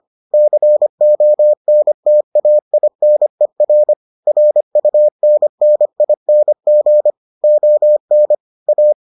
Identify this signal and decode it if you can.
Morse code, heard as a keyed tone, about 25 wpm, 600 Hz. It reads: CONTAINER RUNNING ON A